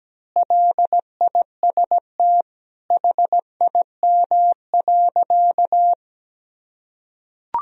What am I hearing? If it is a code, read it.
Morse code, heard as a keyed tone, about 17 wpm, 700 Hz. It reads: LIST HIM.
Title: Sentences from Top 1000 Words - Review Part 2 - 17wpm